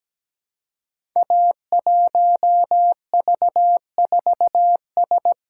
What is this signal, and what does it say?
Morse code, heard as a keyed tone, about 17 wpm, 700 Hz. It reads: A1V4S
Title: Code Groups - 5 Letter:Numbers - 17wpm